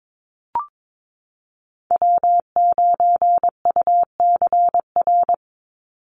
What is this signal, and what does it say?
Morse code, heard as a keyed tone, about 22 wpm, 700 Hz. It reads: W9UCR
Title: Call Signs Repeated 3x - 22wpm